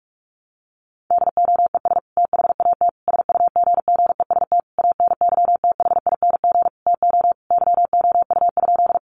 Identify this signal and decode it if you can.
Morse code, heard as a keyed tone, about 44 wpm, 700 Hz. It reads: DOES THAT SUGGEST ANYTHING TO YOU?
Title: Sentences from Top 1300 Words 45wpm